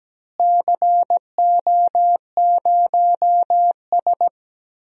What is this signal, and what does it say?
Morse code, heard as a keyed tone, about 17 wpm, 700 Hz. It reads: CO0S
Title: DXCC-100 Call Signs - 4 Characters - Repeated 3x - 17wpm